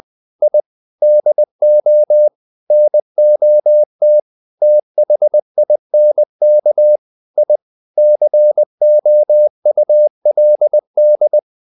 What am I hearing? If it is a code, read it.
Morse code, heard as a keyed tone, about 20 wpm, 600 Hz. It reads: I DO NOT THINK I COULD
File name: the old man and the sea / the old man and the sea part 1 20wpm